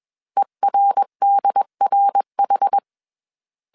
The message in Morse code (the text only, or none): ERDRH